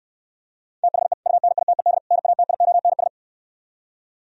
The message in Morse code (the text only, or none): THE UNITED NATIONS